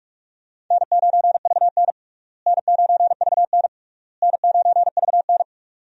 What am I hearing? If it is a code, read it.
Morse code, heard as a keyed tone, about 41 wpm, 700 Hz. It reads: N9VN N9VN N9VN
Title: USA Call Signs - 4 Characters - Repeated 3x - 45wpm